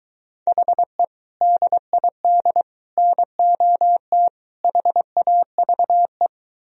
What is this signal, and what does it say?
Morse code, heard as a keyed tone, about 23 wpm, 700 Hz. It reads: HE DID NOT HAVE